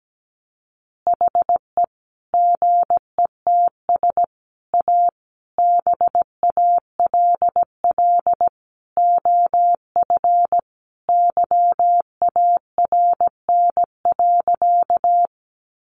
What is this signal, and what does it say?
Morse code, heard as a keyed tone, about 17 wpm, 700 Hz. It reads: HE GETS A BALL OF YARN.
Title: Additional Easy Phrases 17wpm